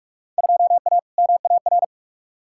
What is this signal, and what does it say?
Morse code, heard as a keyed tone, about 41 wpm, 700 Hz. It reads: 2A MAR
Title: Field Day Exchanges - 45wpm